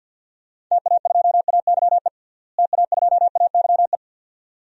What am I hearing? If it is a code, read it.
Morse code, heard as a keyed tone, about 53 wpm, 700 Hz. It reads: TA2AYE TA2AYE
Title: DXCC-100 Call Signs - 6 Characters - Repeated 2x - 50wpm